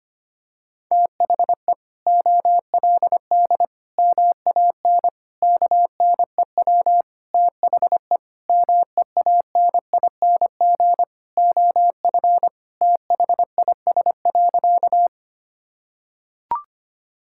Morse code, heard as a keyed tone, about 25 wpm, 700 Hz. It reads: THE OLD MAN KNEW THE MEANING OF THIS.
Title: Sentences from Top 1000 Words - Review Part 6 - 25wpm